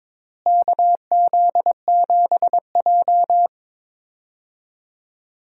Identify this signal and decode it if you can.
Morse code, heard as a keyed tone, about 22 wpm, 700 Hz. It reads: KZ7J
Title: USA Call Signs - 4 Characters - Repeated 3x - 22wpm